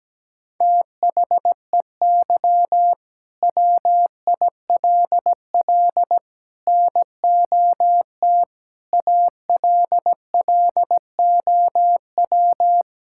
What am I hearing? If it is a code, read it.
Morse code, heard as a keyed tone, about 17 wpm, 700 Hz. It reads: THEY WILL NOT ALLOW
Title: Sentences from Top 1000 Words - Review Part 5 - 17wpm